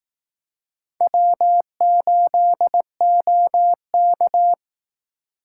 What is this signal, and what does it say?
Morse code, heard as a keyed tone, about 19 wpm, 700 Hz. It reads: W8OK